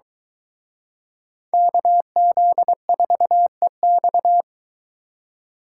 Morse code, heard as a keyed tone, about 23 wpm, 700 Hz. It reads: KZ4EX